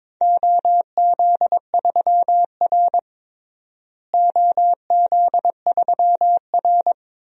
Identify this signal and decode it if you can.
Morse code, heard as a keyed tone, about 22 wpm, 700 Hz. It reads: OZ3R OZ3R
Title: DXCC-100 Call Signs - 4 Characters - Repeated 2x - 22wpm